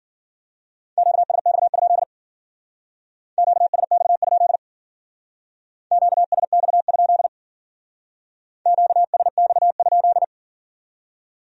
Morse code, heard as a keyed tone, about 53 wpm, 700 Hz. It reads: QSX? QSX? QSX? QSX?